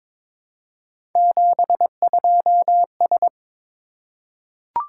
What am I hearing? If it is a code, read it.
Morse code, heard as a keyed tone, about 22 wpm, 700 Hz. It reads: 72S
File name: practice / 22wpm / Three Letter-Number Combinations - 22wpm